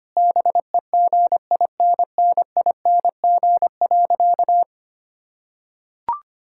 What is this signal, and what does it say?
Morse code, heard as a keyed tone, about 25 wpm, 700 Hz. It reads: BEGINNING.
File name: practice / 25wpm / Sentences with 3-Word Phrases - 25wpm